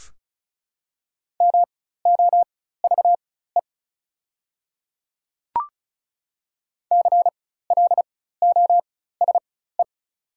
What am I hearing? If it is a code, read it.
Morse code, heard as a keyed tone, about 33 wpm, spaced as for 16 wpm, 690 Hz. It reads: MOVE CLOSE
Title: 4x Character Spacing - Top 500 Words - 35wpm